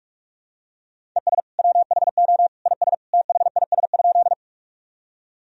Morse code, heard as a keyed tone, about 46 wpm, 700 Hz. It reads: ES WHO IS THIS?
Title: Sentences from Top 100 Words and Abbreviations 45wpm